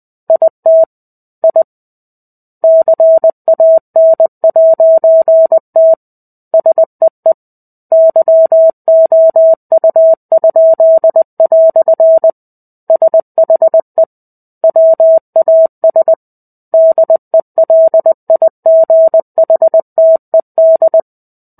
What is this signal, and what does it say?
Morse code, heard as a keyed tone, about 20 wpm, 650 Hz. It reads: IT I CAN'T SEE YOU?" SHE WAS DELIGHTED
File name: 05-Chapter-05 20 WPM